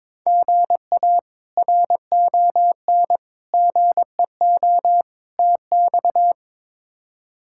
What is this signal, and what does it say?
Morse code, heard as a keyed tone, about 22 wpm, 700 Hz. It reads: GA RON GEO TX
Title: K1USN SST QSO Practice - 22wpm